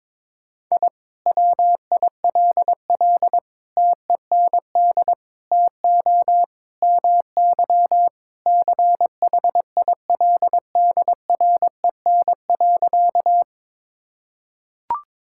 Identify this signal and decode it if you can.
Morse code, heard as a keyed tone, about 22 wpm, 700 Hz. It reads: I WILL TEND TO MY CHILDREN.
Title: Sentences from Top 1000 Words - Review Part 5 - 22wpm